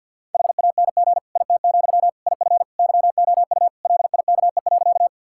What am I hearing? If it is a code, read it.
Morse code, heard as a keyed tone, about 50 wpm, 700 Hz. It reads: HANG IT, IF YOU LIKE.
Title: Sentences from Top 1600 Words 50wpm